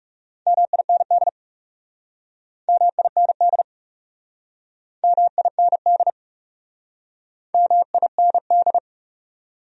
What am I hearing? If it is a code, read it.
Morse code, heard as a keyed tone, about 40 wpm, 700 Hz. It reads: MIND MIND MIND MIND